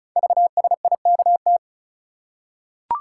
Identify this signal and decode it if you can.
Morse code, heard as a keyed tone, about 36 wpm, 700 Hz. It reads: VSIKT
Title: Code Groups - 5 Letters - 35wpm